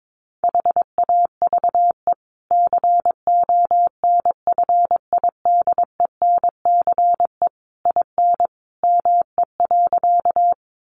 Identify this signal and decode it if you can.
Morse code, heard as a keyed tone, about 22 wpm, 700 Hz. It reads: HAVE CONFIDENCE IN ME.
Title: Sentences from Top 1500 Words 22wpm